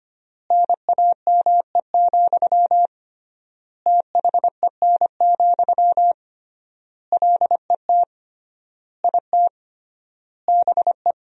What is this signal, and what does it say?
Morse code, heard as a keyed tone, about 25 wpm, 700 Hz. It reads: NAME, THEN, LET IT BE
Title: 3x Word-Spacing - Sentences from Top 500 Words - 25wpm